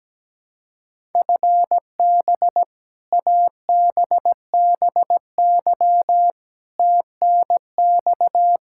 Morse code, heard as a keyed tone, about 17 wpm, 700 Hz. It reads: FB ABBY TNX
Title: Practice QSO - Exchange 3 - More Info - 17wpm